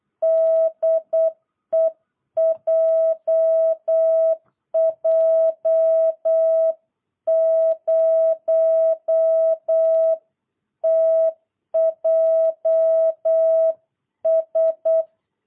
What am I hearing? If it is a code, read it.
Morse code, heard as a keyed tone, about 8 wpm, 650 Hz. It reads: DE JJ0TJS
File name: jh8cbh / 2022cw